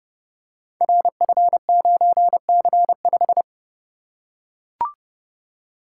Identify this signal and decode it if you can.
Morse code, heard as a keyed tone, about 30 wpm, 700 Hz. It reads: RF9C5
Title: Code Groups - 5 Letter:Numbers - 30wpm